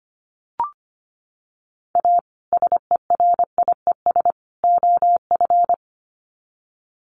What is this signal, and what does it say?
Morse code, heard as a keyed tone, about 25 wpm, 700 Hz. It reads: A SERIES OF
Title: Sets of 3 Words 25wpm